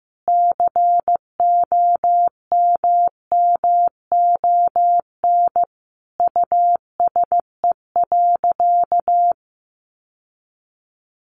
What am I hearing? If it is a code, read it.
Morse code, heard as a keyed tone, about 15 wpm, 700 Hz. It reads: COMMON USE.
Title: Sentences from Top 1300 Words 15wpm